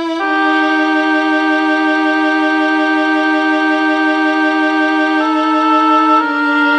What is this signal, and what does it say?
There is no code.